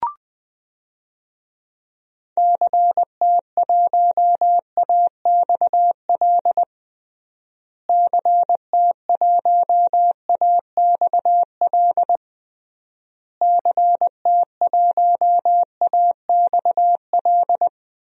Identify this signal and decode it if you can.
Morse code, heard as a keyed tone, about 21 wpm, 700 Hz. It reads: CT1AXL CT1AXL CT1AXL
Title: DXCC-100 Call Signs - 6 Characters - Repeated 3x - 20wpm